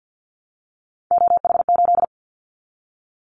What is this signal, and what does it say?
Morse code, heard as a keyed tone, about 50 wpm, 700 Hz. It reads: OH8
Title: Callsign Prefixes for Top 25 DXCC - Repeated 3x - 50wpm